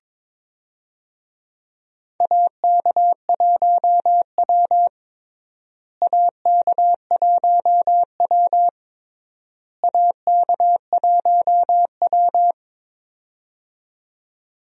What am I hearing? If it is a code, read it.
Morse code, heard as a keyed tone, about 22 wpm, 700 Hz. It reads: AK1W AK1W AK1W
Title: USA Call Signs - 4 Characters - Repeated 3x - 22wpm